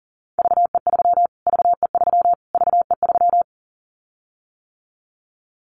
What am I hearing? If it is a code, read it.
Morse code, heard as a keyed tone, about 38 wpm, 700 Hz. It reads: VE3 VE3 VE3
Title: Callsign Prefixes for Top 25 DXCC - Repeated 3x - 40wpm